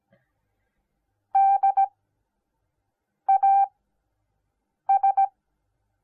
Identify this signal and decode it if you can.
Morse code, heard as a keyed tone, about 17 wpm, spaced as for 6 wpm, 780 Hz. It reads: DAS